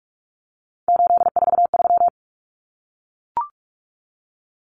Morse code, heard as a keyed tone, about 44 wpm, 700 Hz. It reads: 843
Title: Common Cut Numbers - 3-Digit 45wpm